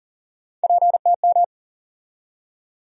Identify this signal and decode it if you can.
Morse code, heard as a keyed tone, about 40 wpm, 700 Hz. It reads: PTM
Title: USA Call Sign Postfixes - 3 Letters 40wpm